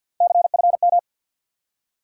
Morse code, heard as a keyed tone, about 49 wpm, 700 Hz. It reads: CFM
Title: Top 90 QSO Elements 50wpm